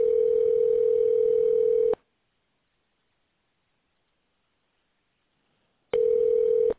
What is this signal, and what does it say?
Ringback tone (North America), 440 + 480 Hz.